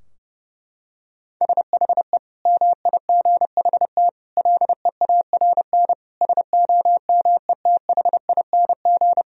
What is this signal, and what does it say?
Morse code, heard as a keyed tone, about 30 wpm, 700 Hz. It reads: SHE MIGHT LEARN SOMETHING
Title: Sentences from Top 1000 Words - Review Part 4 - 30wpm